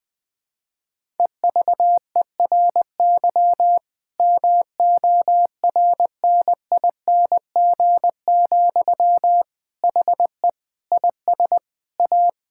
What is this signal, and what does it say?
Morse code, heard as a keyed tone, about 20 wpm, 700 Hz. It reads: EVERY MORNING, HE IS A
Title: Sentences from Top 1000 Words - Review Part 6 - 20wpm